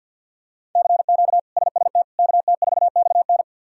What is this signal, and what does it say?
Morse code, heard as a keyed tone, about 46 wpm, 690 Hz. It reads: CQ SST KT4XN
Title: K1USN SST QSO Practice - 50wpm